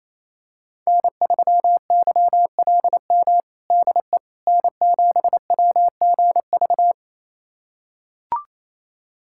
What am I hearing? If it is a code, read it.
Morse code, heard as a keyed tone, about 27 wpm, 700 Hz. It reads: N3YLM DE N7WGV
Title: USA Callsign DE Callsign - 5 Characters - 28wpm